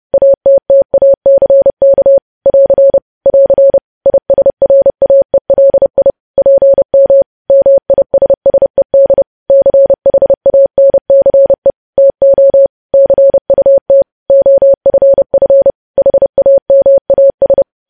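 Morse code, heard as a keyed tone, about 30 wpm, 550 Hz. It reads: ATTACK <AR> <AR> ISRAELI PM MISSED CHANCE TO CUT OFF HAMAS
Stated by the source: News headlines sent as Morse practice at 30.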